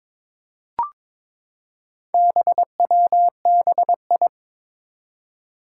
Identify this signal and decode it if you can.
Morse code, heard as a keyed tone, about 22 wpm, 700 Hz. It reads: BWBI
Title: Four Challenging Letter-Number Combinations 22wpm